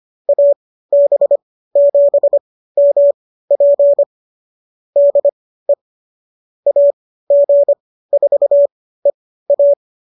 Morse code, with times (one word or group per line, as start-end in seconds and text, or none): 0.3-4.0 AB7MP
5.0-5.7 DE
6.7-9.7 AG4EA